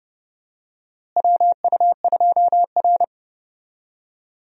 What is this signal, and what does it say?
Morse code, heard as a keyed tone, about 30 wpm, 700 Hz. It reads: WU2R
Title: USA Call Signs - 4 Characters - Repeated 2x - 30wpm